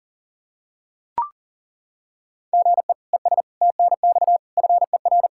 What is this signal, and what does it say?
Morse code, heard as a keyed tone, about 40 wpm, 700 Hz. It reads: GE ES TNX FER
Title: Practice QSO - Exchange 1 - Beginning - 40wpm